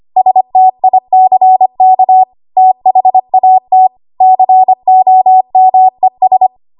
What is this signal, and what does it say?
Morse code, heard as a keyed tone, about 25 wpm, 750 Hz. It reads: STICK THAT COMES
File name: Archive / 25 WPM / 25WPM